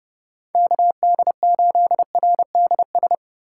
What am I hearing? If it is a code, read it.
Morse code, heard as a keyed tone, about 29 wpm, 700 Hz. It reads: KD8RDS